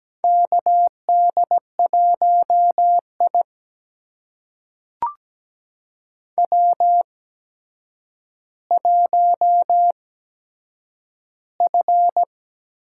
Morse code, heard as a keyed tone, about 17 wpm, spaced as for 8 wpm, 700 Hz. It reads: KD1I W1F
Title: USA Call Signs - 4 Characters - 8x Character Spacing - 17wpm